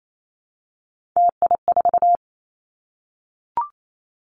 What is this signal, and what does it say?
Morse code, heard as a keyed tone, about 28 wpm, 700 Hz. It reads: TI4
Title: Callsign Prefixes for Top 100 DXCC - 28wpm